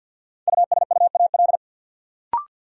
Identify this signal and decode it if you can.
Morse code, heard as a keyed tone, about 53 wpm, 700 Hz. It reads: USUAL
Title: Sentences from Top 400 Words - New Word Drill - 50wpm